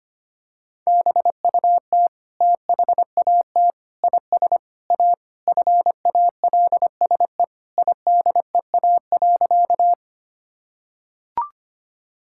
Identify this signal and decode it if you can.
Morse code, heard as a keyed tone, about 25 wpm, 700 Hz. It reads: BUT THAT IS A FALSE IDEA.
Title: Sentences from Top 1600 Words 25wpm